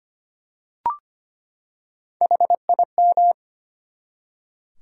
Morse code, heard as a keyed tone, about 25 wpm, 700 Hz. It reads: HIM
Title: Top 100 Words 25wpm